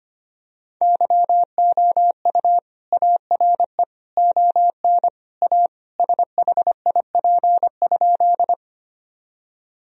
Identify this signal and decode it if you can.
Morse code, heard as a keyed tone, about 25 wpm, 700 Hz. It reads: YOU ARE ON A SHIP?
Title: Sentences from Top 1000 Words - Review Part 2 - 25wpm